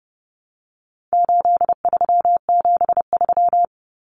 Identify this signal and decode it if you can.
Morse code, heard as a keyed tone, about 30 wpm, 700 Hz. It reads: 8373
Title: Numbers - 4-Digit 30wpm